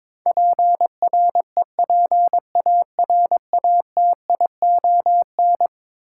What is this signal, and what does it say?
Morse code, heard as a keyed tone, about 22 wpm, 700 Hz. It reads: PREPARATION
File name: practice / 22wpm / Eleven Letter Words - 22wpm